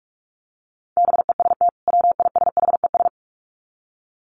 Morse code, heard as a keyed tone, about 44 wpm, 700 Hz. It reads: BEST WISHES
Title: Common QSO Phrases 45wpm